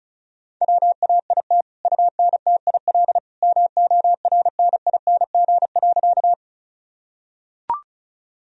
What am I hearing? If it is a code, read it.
Morse code, heard as a keyed tone, about 36 wpm, 700 Hz. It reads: WAIT UNTIL MORNING.
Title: Sentences from Top 1000 Words - Review Part 1 - 35wpm